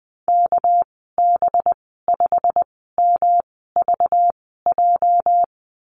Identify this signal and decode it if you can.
Morse code, heard as a keyed tone, about 20 wpm, spaced as for 14 wpm, 700 Hz. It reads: KB5MVJ
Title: 219 Lesson 35g - Number 0 - Call Signs 20wpm